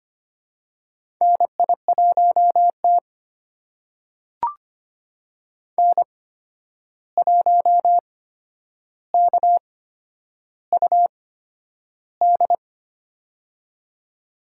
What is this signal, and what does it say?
Morse code, heard as a keyed tone, about 25 wpm, spaced as for 10 wpm, 700 Hz. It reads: NI1T N1KUD
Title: USA Call Signs - 8x Character Spacing - 25wpm